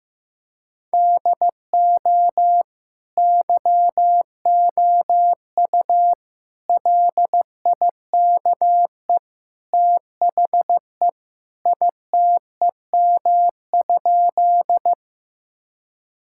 Morse code, heard as a keyed tone, about 16 wpm, 700 Hz. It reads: DO YOU LIKE THE ITEM?
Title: Sentences from Top 1000 Words - Review Part 6 - 15wpm